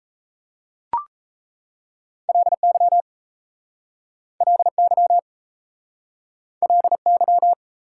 Morse code, heard as a keyed tone, about 37 wpm, 690 Hz. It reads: LY LY LY